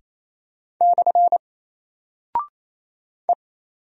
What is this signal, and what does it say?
Morse code, heard as a keyed tone, about 28 wpm, 700 Hz. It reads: / E
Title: Single Character (Letter:Number:Punctuation:Prosign) Alt 28wpm